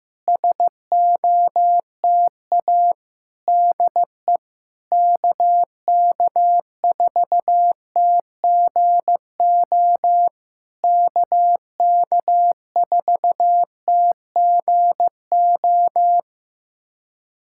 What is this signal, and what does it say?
Morse code, heard as a keyed tone, about 14 wpm, 690 Hz. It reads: SOTA DE KK4TGO KK4TGO
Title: SOTA QSO Practice 15wpm